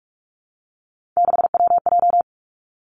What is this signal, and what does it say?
Morse code, heard as a keyed tone, about 46 wpm, 700 Hz. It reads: 6WJ
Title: Three Challenging Letter-Number Combinations 45wpm